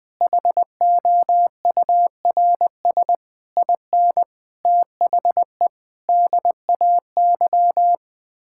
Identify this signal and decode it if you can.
Morse code, heard as a keyed tone, about 20 wpm, 700 Hz. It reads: HOURS IN THE DAY
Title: Sets of 9 Words - 20wpm